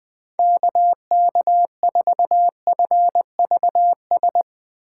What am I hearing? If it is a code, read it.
Morse code, heard as a keyed tone, about 20 wpm, 700 Hz. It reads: KK4FVS
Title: USA Call Signs - 3x Character Spacing - 20wpm